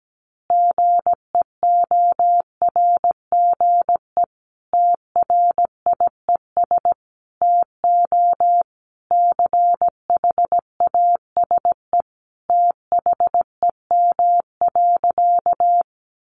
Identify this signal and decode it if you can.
Morse code, heard as a keyed tone, about 17 wpm, 700 Hz. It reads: GEORGE TRIES TO CHASE THEM.
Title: Additional Easy Phrases 17wpm